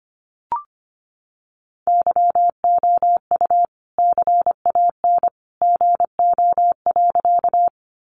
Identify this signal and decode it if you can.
Morse code, heard as a keyed tone, about 25 wpm, 700 Hz. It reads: YOU CAN GO.
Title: Sentences from Top 100 Words 25wpm